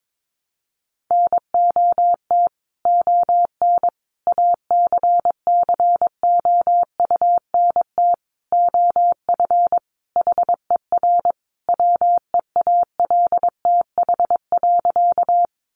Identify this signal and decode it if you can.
Morse code, heard as a keyed tone, about 22 wpm, 700 Hz. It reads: NOT ON ACCOUNT OF HER WEALTH.
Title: Sentences from Top 1200 Words 22wpm